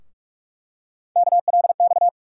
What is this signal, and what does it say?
Morse code, heard as a keyed tone, about 45 wpm, 700 Hz. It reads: KLX
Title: USA Call Sign Postfixes - 3 Letters 45wpm